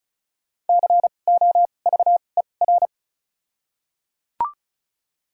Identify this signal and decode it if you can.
Morse code, heard as a keyed tone, about 35 wpm, spaced as for 25 wpm, 700 Hz. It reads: COVER
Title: 2x Character Spacing - Top 300 Words - 35wpm